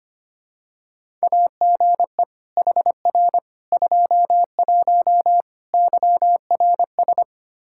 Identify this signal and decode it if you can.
Morse code, heard as a keyed tone, about 25 wpm, 700 Hz. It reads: AGE HR 21 YRS
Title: K1USN SST QSO Practice 25wpm